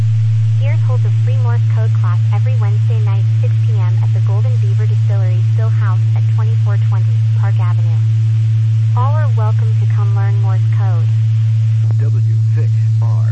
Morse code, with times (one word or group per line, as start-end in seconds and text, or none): none